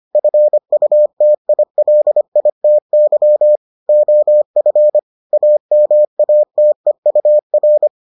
Morse code, heard as a keyed tone, about 25 wpm, 600 Hz. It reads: FUTILITY OF AMATEUR